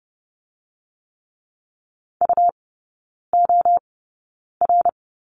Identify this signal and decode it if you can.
Morse code, heard as a keyed tone, about 30 wpm, spaced as for 9 wpm, 690 Hz. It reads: UOR
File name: Lesson 17e - Letter u - Three Characters 30wpm